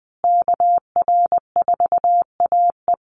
Morse code, heard as a keyed tone, about 20 wpm, 700 Hz. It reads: KR4AE
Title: Call Signs Repeated 2x - 20wpm